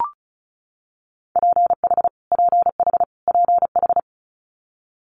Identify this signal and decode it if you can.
Morse code, heard as a keyed tone, about 36 wpm, 700 Hz. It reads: PH PH PH
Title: Multi-letter Phonemes Repeated 3x 35wpm